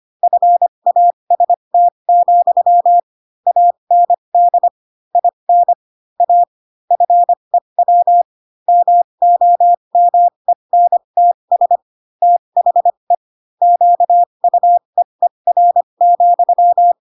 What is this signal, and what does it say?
Morse code, heard as a keyed tone, about 25 wpm, spaced as for 22 wpm, 700 Hz. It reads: FAST, AND IN A FEW MOMENTS THE QUEER,